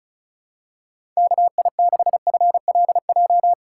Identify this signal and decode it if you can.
Morse code, heard as a keyed tone, about 36 wpm, 700 Hz. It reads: KI6FLJ
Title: USA Call Signs - 6 Characters - 4x Character Spacing - 35wpm